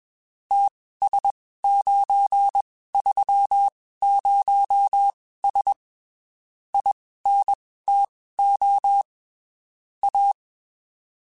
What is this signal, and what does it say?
Morse code, heard as a keyed tone, about 21 wpm, spaced as for 15 wpm, 800 Hz. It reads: TS930S INTO A
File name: (spaced) / QSOs Spaced-mp3s / WA7QOX 2 @20 sp